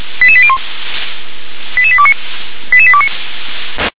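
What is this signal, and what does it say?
Five-tone selective call (ZVEI), heard as quick runs of five tones, tones of about 70 ms. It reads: 88801 88829 88829
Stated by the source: ZVEI five-tone call received by SDR.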